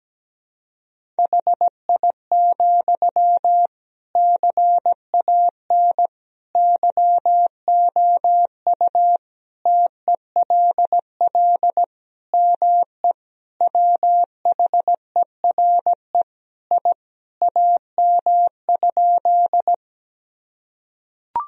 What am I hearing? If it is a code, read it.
Morse code, heard as a keyed tone, about 17 wpm, 700 Hz. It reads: HI, CAN YOU TELL ME WHERE I AM?
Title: Sentences from Top 1000 Words - Review Part 3 - 17wpm